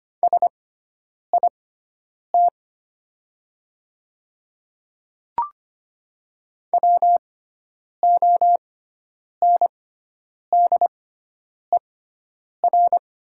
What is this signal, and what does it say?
Morse code, heard as a keyed tone, about 25 wpm, spaced as for 9 wpm, 700 Hz. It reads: SIT WONDER
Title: 6x Character Spacing - Top 500 Words - 25wpm